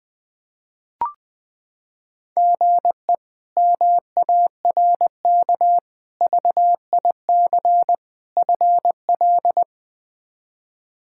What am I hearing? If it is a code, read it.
Morse code, heard as a keyed tone, about 20 wpm, 700 Hz. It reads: GE MARK VIC FL
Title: K1USN SST QSO Practice - 20wpm